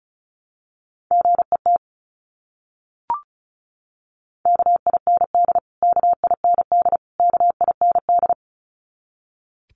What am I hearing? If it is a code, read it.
Morse code, heard as a keyed tone, about 34 wpm, 700 Hz. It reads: GET KIND KIND KIND
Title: Top 500 Words Repeated 35wpm